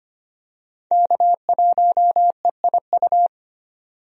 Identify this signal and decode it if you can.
Morse code, heard as a keyed tone, about 25 wpm, 700 Hz. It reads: K1EIU
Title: USA Call Signs - 5 Characters - 25wpm